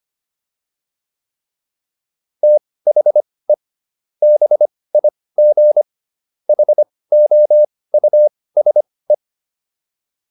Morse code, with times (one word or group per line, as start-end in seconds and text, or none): 2.4-3.5 THE
4.2-5.8 BIG
6.5-9.1 HOUSE